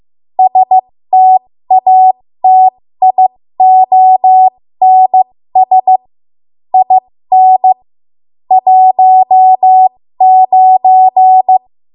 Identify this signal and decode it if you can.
Morse code, heard as a keyed tone, about 15 wpm, spaced as for 13 wpm, 750 Hz. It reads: STATIONS IN 19
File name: Morse / 13WPM